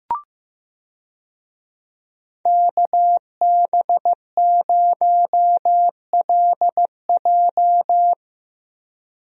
Morse code, heard as a keyed tone, about 15 wpm, 700 Hz. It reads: KB0LJ
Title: USA Call Signs - 5 Characters - Repeated 2x - 15wpm